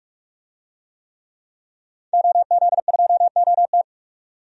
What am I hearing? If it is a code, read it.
Morse code, heard as a keyed tone, about 42 wpm, 700 Hz. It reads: OZ2OT
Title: DXCC-100 Call Signs - 5 Characters - 45wpm